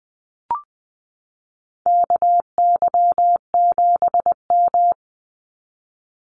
Morse code, heard as a keyed tone, about 20 wpm, 700 Hz. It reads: KY7M
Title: Call Signs Repeated 3x - 20wpm